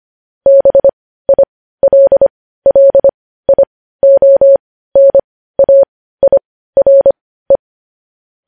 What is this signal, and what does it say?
Morse code, heard as a keyed tone, about 25 wpm, spaced as for 15 wpm, 550 Hz. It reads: BILLIONAIRE